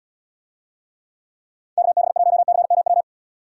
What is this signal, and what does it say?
Morse code, heard as a keyed tone, about 77 wpm, 700 Hz. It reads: KB1CMW